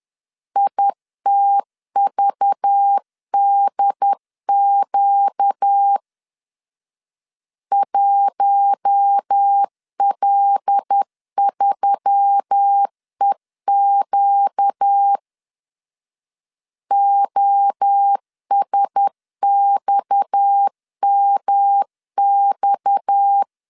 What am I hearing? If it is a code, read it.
Morse code, heard as a keyed tone, about 10 wpm, 800 Hz. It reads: ITVDQ 1L3EQ OSXMX
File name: LEZIONI / 6 PKX0 / 6 PKX 10wpm 2426